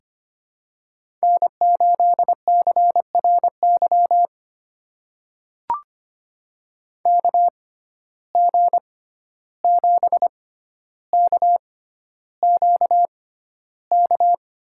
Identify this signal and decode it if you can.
Morse code, heard as a keyed tone, about 25 wpm, spaced as for 12 wpm, 700 Hz. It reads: N8CRY KG7KQK
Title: USA Call Signs - 6x Character Spacing - 25wpm